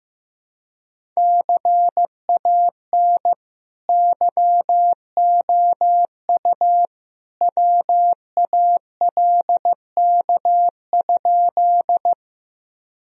Morse code, heard as a keyed tone, about 15 wpm, 700 Hz. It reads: CAN YOU WALK?
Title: Sentences from Top 1000 Words - Review Part 1 - 15wpm